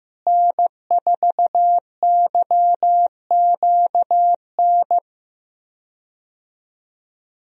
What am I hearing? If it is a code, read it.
Morse code, heard as a keyed tone, about 15 wpm, 700 Hz. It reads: N4YQN
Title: USA Call Signs - 5 Characters - Repeated 3x - 15wpm